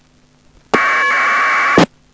Packet radio (AX.25, as in APRS).
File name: uploader / iss aprs